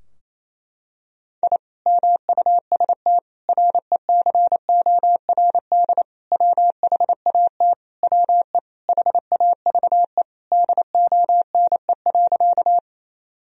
Morse code, heard as a keyed tone, about 27 wpm, 700 Hz. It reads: I MUST RECORD WHAT WE HAVE DONE.
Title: Sentences from Top 1000 Words - Review Part 3 - 28wpm